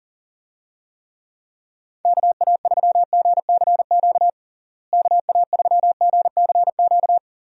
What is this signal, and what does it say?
Morse code, heard as a keyed tone, about 40 wpm, 700 Hz. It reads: KA3GCQ KA3GCQ
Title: USA Call Signs - 6 Characters - Repeated 3x - 40wpm